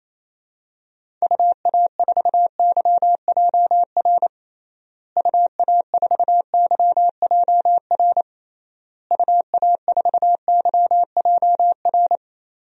Morse code, heard as a keyed tone, about 28 wpm, 700 Hz. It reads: UA4YJR UA4YJR UA4YJR
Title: DXCC-100 Call Signs - 6 Characters - Repeated 3x - 28wpm